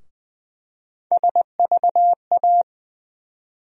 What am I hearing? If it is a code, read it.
Morse code, heard as a keyed tone, about 20 wpm, 700 Hz. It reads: SVA
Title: USA Call Sign Postfixes - 3 Letters 20wpm